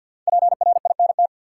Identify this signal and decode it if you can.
Morse code, heard as a keyed tone, about 53 wpm, 700 Hz. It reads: PRINT
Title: Sentences from Top 1400 Words - New Word Drill - 50wpm